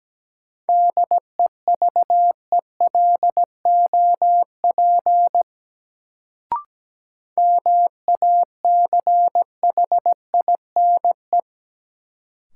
Morse code, heard as a keyed tone, about 17 wpm, 700 Hz. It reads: DEVELOP MACHINE
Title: Sentences from Top 1000 Words - Review Part 2 - New Word Drill - 17wpm